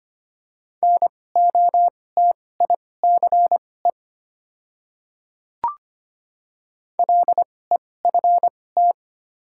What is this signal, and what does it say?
Morse code, heard as a keyed tone, about 25 wpm, spaced as for 18 wpm, 700 Hz. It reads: NOTICE LEFT
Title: 2x Character Spacing - Top 500 Words - 25wpm